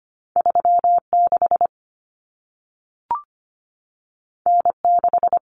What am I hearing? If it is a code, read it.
Morse code, heard as a keyed tone, about 25 wpm, 700 Hz. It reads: 36 N6
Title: Common Cut Numbers - 2-Digit 25wpm